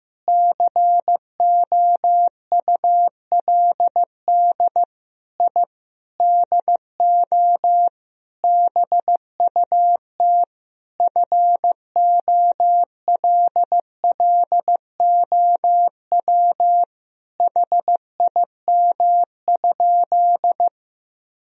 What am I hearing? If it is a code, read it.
Morse code, heard as a keyed tone, about 16 wpm, 700 Hz. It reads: COULD I DO BUT FOLLOW HIM?
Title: Sentences from Top 1000 Words - Review Part 1 - 15wpm